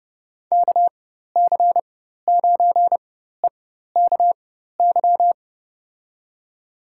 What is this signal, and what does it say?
Morse code, heard as a keyed tone, about 30 wpm, spaced as for 14 wpm, 700 Hz. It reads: KC9EKY
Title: USA Call Signs - 6 Characters - 4x Character Spacing - 30wpm